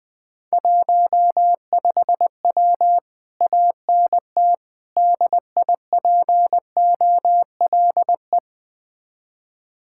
Morse code, heard as a keyed tone, about 20 wpm, 700 Hz. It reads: 15W ANT DIPOLE
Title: Speed-Racing Common QSO Phrases - 17wpm